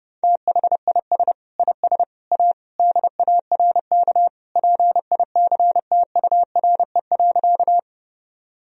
Morse code, heard as a keyed tone, about 30 wpm, 700 Hz. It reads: THIS IS A DARK PICTURE.
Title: Sentences from Top 1000 Words - Review Part 1 - 30wpm